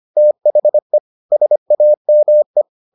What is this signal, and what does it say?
Morse code, heard as a keyed tone, about 25 wpm, 600 Hz. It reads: THE SAME